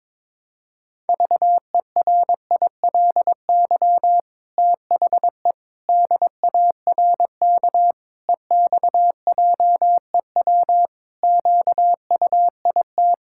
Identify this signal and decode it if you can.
Morse code, heard as a keyed tone, about 21 wpm, 700 Hz. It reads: VERILY THE DARK EXJEW QUIT
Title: Pangram Challenge v2 - 22wpm